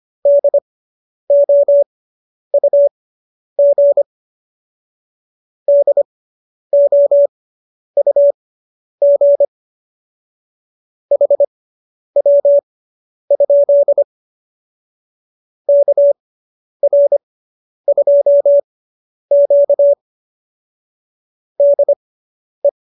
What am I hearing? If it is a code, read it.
Morse code, heard as a keyed tone, about 25 wpm, spaced as for 10 wpm, 570 Hz. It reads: DOUG DOUG HW? KR2Q DE
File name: QSO220 10